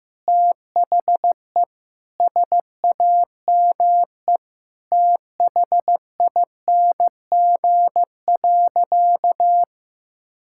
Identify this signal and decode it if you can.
Morse code, heard as a keyed tone, about 15 wpm, 700 Hz. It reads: THE SAME THING.
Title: Sentences from Top 1600 Words 15wpm